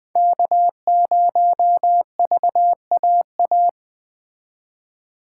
Morse code, heard as a keyed tone, about 20 wpm, 700 Hz. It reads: K0VAA
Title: Speed Racing - USA 1x3 Call Signs 17wpm